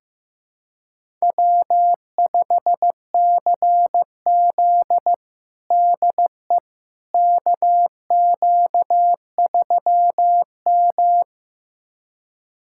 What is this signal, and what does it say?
Morse code, heard as a keyed tone, about 15 wpm, 700 Hz. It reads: W5CZ DE KQ3M
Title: USA Callsign DE Callsign - 4 Characters - 15wpm